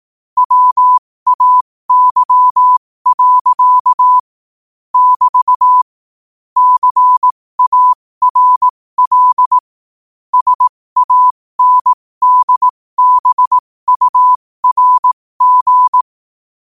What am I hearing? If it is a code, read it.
Morse code, heard as a keyed tone, about 18 wpm, spaced as for 15 wpm, 1000 Hz. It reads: WAY. <BT> CARL SANDBURG
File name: qotdfiles / 15wpm